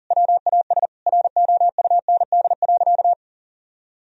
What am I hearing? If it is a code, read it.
Morse code, heard as a keyed tone, about 40 wpm, 700 Hz. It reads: WAS ROUND.
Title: Sentences from Top 1000 Words - Review Part 1 - 40wpm